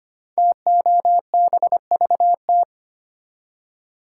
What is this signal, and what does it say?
Morse code, heard as a keyed tone, about 25 wpm, 700 Hz. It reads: TOBVT
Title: Code Groups - 5 Letters - 25wpm